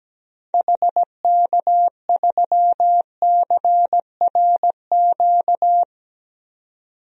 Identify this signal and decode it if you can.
Morse code, heard as a keyed tone, about 17 wpm, 700 Hz. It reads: HK3CRQ
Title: DXCC-100 Call Signs - 6 Characters - 17wpm